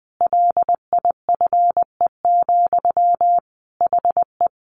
Morse code, heard as a keyed tone, about 20 wpm, 700 Hz. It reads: LIFE, HE
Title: Sentences from Top 900 Words 20wpm